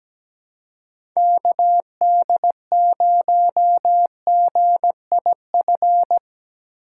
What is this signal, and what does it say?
Morse code, heard as a keyed tone, about 17 wpm, 700 Hz. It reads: KD0GIF
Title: USA Call Signs - 6x Character Spacing - 17wpm